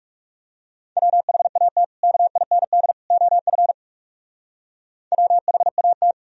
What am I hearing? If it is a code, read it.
Morse code, heard as a keyed tone, about 36 wpm, 700 Hz. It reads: WHAT KIND OF WHAT
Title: Speed-Racing - Sets of 3 Words 30wpm